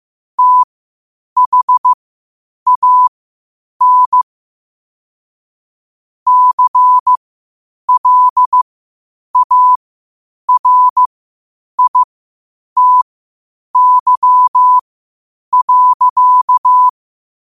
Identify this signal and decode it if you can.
Morse code, heard as a keyed tone, about 15 wpm, spaced as for 8 wpm, 1000 Hz. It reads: THAN CLARITY.
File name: qotdfiles / 8wpm